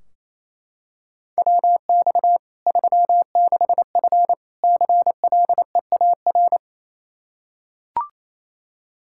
Morse code, heard as a keyed tone, about 28 wpm, 700 Hz. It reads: WX 36F CLEAR
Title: Practice QSO - Exchange 2 - Info - 28wpm